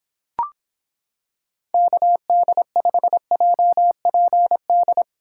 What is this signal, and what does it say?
Morse code, heard as a keyed tone, about 26 wpm, 700 Hz. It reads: KD5JPD